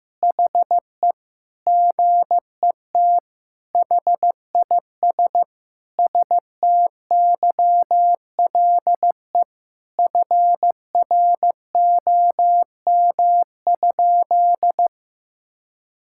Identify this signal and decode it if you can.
Morse code, heard as a keyed tone, about 16 wpm, 700 Hz. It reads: HE GET HIS STYLE FROM?
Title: Sentences from Top 1000 Words - Review Part 5 - 15wpm